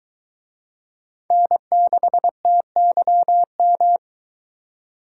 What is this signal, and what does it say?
Morse code, heard as a keyed tone, about 22 wpm, 700 Hz. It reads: N6TYM